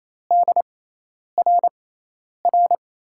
Morse code, heard as a keyed tone, about 28 wpm, spaced as for 10 wpm, 700 Hz. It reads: DRR
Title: USA Call Signs - 5 Characters - 6x Character Spacing - 28wpm